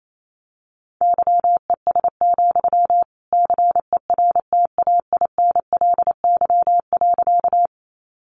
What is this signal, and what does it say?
Morse code, heard as a keyed tone, about 28 wpm, 700 Hz. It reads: YES, CERTAINLY.
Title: Sentences from Top 900 Words 28wpm